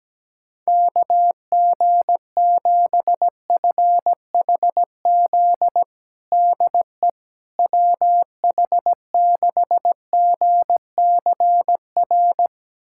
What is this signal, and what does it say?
Morse code, heard as a keyed tone, about 17 wpm, 700 Hz. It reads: KG7FHZ DE WH6GCR
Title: USA Callsign DE Callsign - 6 Characters - 17wpm